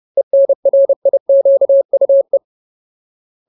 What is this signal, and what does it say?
Morse code, heard as a keyed tone, about 30 wpm, 550 Hz. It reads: ENRIQUE
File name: Ultimate Word List / 30WPM Ultimate